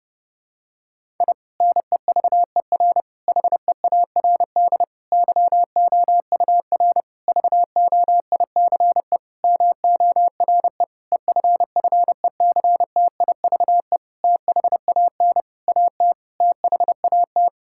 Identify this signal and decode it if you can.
Morse code, heard as a keyed tone, about 30 wpm, 700 Hz. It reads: I NEVER HEARD YOUR VOICE MORE EFFECTIVE THAN AT THAT
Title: Sentences from Top 1000 Words - Review Part 6 - 30wpm